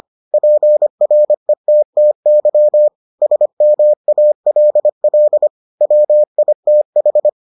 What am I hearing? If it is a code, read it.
Morse code, heard as a keyed tone, about 25 wpm, 600 Hz. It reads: PRETTY SMALL WITH